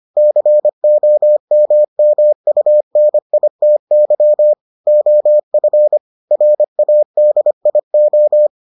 Morse code, heard as a keyed tone, about 25 wpm, 600 Hz. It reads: COMMUNITY OF RADIO